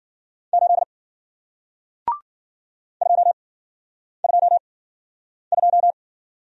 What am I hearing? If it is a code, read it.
Morse code, heard as a keyed tone, about 53 wpm, 700 Hz. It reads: 8 2 2 2